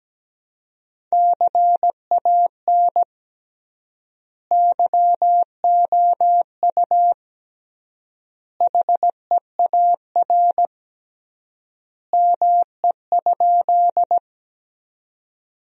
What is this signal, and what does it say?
Morse code, heard as a keyed tone, about 17 wpm, 700 Hz. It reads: CAN YOU HEAR ME?
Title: 3x Word-Spacing - Sentences from Top 300 Words - 17wpm